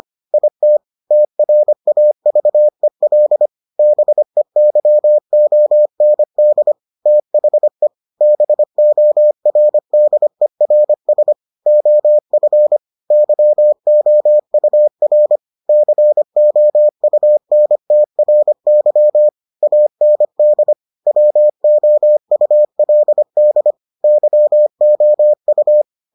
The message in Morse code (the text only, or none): IT TRAVEL BEYOND THE BORDERS OF YOUR COUNTRY AND WOULD YOU